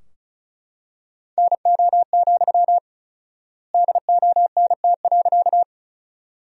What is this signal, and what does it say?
Morse code, heard as a keyed tone, about 34 wpm, 700 Hz. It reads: NO, DONT.
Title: 4x Word-Spacing - Sentences from Top 300 Words - 35wpm